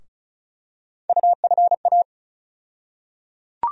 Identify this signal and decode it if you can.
Morse code, heard as a keyed tone, about 34 wpm, 700 Hz. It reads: UFA
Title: USA Call Sign Postfixes - 3 Letters 35wpm